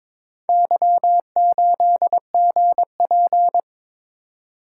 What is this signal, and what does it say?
Morse code, heard as a keyed tone, about 22 wpm, 700 Hz. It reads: Y8GP
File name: practice / 22wpm / Four Letter-Number Combinations - 22wpm